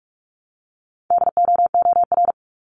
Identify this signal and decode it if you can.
Morse code, heard as a keyed tone, about 46 wpm, 700 Hz. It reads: DOOR
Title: Multi-letter Phonemes Repeated 3x 45wpm